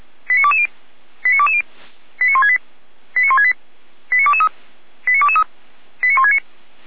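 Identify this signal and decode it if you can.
Five-tone selective call (ZVEI), heard as quick runs of five tones, tones of about 70 ms. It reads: 89220 89220 89168 89168 89223 89223 89169
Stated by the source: ZVEI five-tone selective call received by SDR, recorded by Feuerwehr Orlamünde (fire brigade).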